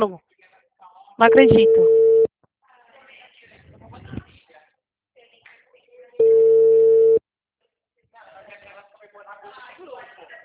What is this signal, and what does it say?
Ringback tone (China, 450 Hz), with speech.